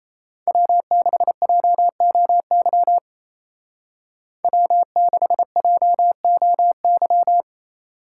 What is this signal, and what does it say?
Morse code, heard as a keyed tone, about 34 wpm, 700 Hz. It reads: W6JOY W6JOY
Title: Speed Racing - USA 1x3 Call Signs 28wpm